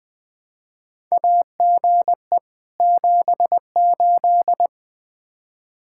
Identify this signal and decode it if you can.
Morse code, heard as a keyed tone, about 20 wpm, 700 Hz. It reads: AGE 78
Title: K1USN SST QSO Practice 20wpm